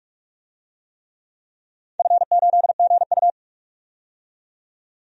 Morse code, heard as a keyed tone, about 44 wpm, 700 Hz. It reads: F8GU
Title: DXCC-100 Call Signs - 4 Characters - 45wpm